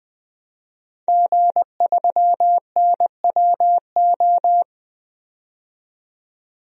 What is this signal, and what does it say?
Morse code, heard as a keyed tone, about 20 wpm, 700 Hz. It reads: G3NWO
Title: DXCC-100 Call Signs - 5 Characters - 20wpm